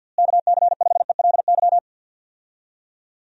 Morse code, heard as a keyed tone, about 46 wpm, 700 Hz. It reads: KC5ELY
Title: USA Call Signs - 6 Characters - Repeated 2x - 50wpm